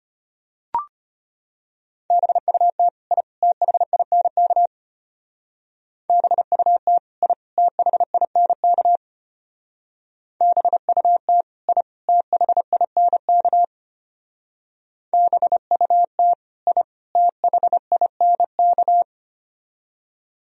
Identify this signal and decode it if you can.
Morse code, heard as a keyed tone, about 30 wpm, 700 Hz. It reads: BUT I THINK BUT I THINK BUT I THINK BUT I THINK
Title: Speed-Racing - Sets of 3 Words 25wpm